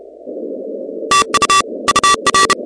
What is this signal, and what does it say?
Morse code, heard as a keyed tone, about 31 wpm, 1160 Hz, with noise. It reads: TU UR
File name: contests / 2019-cw-wpx-kt5j-k5pi- -2330 0-16